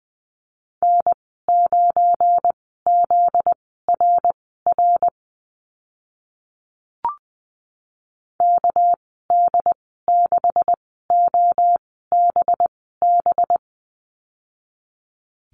Morse code, heard as a keyed tone, about 20 wpm, spaced as for 14 wpm, 700 Hz. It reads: N9ZRR KD6OBB
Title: 245 Lesson 39g - Letter z - Call Signs 20wpm